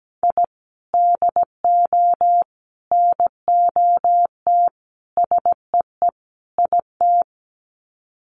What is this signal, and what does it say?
Morse code, heard as a keyed tone, about 17 wpm, 700 Hz. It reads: I DO NOT SEE IT
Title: Sets of 5 Words Encore2 17wpm